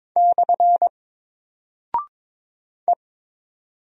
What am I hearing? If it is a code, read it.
Morse code, heard as a keyed tone, about 22 wpm, 700 Hz. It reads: / E
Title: Single Character (Letter:Number:Punctuation:Prosign) Alt 22wpm